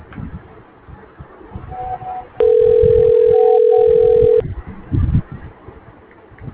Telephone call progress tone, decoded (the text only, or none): ringback tone